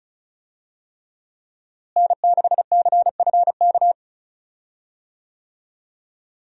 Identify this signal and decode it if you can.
Morse code, heard as a keyed tone, about 36 wpm, 700 Hz. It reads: N6CFK
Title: USA Call Signs - 5 Characters - Repeated 3x - 35wpm